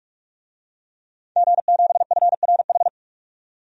Morse code, heard as a keyed tone, about 46 wpm, 700 Hz. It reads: G7FRH
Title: DXCC-100 Call Signs - 5 Characters - 45wpm